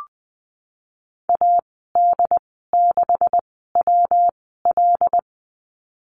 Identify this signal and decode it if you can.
Morse code, heard as a keyed tone, about 20 wpm, spaced as for 14 wpm, 700 Hz. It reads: AD6WL
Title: 193 Lesson 31g - Letter v - Call Signs 20wpm